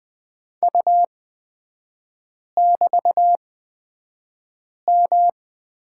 Morse code, heard as a keyed tone, about 20 wpm, spaced as for 5 wpm, 700 Hz. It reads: U<BT>M
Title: 243 Lesson 39e - Letter z - Three Characters 20wpm